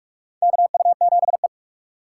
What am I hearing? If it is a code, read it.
Morse code, heard as a keyed tone, about 46 wpm, 700 Hz. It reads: KU7E
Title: USA Call Signs - 4 Characters - 4x Character Spacing - 45wpm